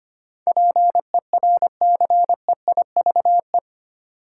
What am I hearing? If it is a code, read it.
Morse code, heard as a keyed tone, about 25 wpm, 700 Hz. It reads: PERCEIVE
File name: 8 Letter Words - First 3 Letters Given - 25wpm